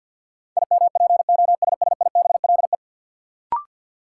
Morse code, heard as a keyed tone, about 47 wpm, 700 Hz. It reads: IMPOSSIBLE
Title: Sentences from Top 1100 Words - New Word Drill Alt - 50wpm